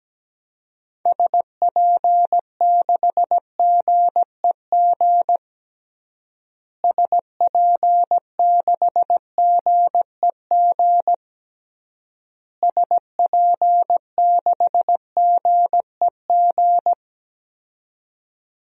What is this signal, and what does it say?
Morse code, heard as a keyed tone, about 17 wpm, 700 Hz. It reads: SP6GEG SP6GEG SP6GEG
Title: DXCC-100 Call Signs - 6 Characters - Repeated 3x - 17wpm